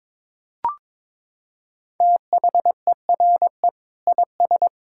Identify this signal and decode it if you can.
Morse code, heard as a keyed tone, about 22 wpm, 700 Hz. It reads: THERE IS
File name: practice / 22wpm / Sets of 7 Words - 22wpm